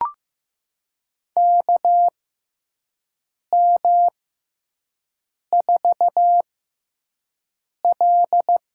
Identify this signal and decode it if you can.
Morse code, heard as a keyed tone, about 15 wpm, spaced as for 5 wpm, 700 Hz. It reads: KM4L
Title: USA Call Signs - 6x Character Spacing - 15wpm